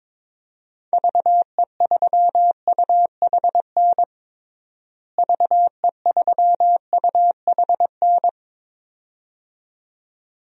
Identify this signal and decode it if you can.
Morse code, heard as a keyed tone, about 22 wpm, 700 Hz. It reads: VE3UHN VE3UHN
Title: DXCC-100 Call Signs - 6 Characters - Repeated 2x - 22wpm